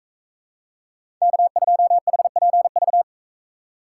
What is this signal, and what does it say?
Morse code, heard as a keyed tone, about 44 wpm, 700 Hz. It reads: K2HPV